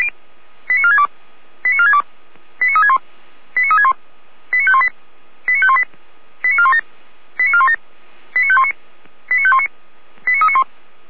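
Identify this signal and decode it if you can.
Five-tone selective call (ZVEI), heard as quick runs of five tones, tones of about 70 ms. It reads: 89462 89462 89361 89361 89518 89518 89417 89417 89519 89519 89391
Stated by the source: ZVEI five-tone selective call received by SDR, feuerwehr-orlamuende fire brigade server.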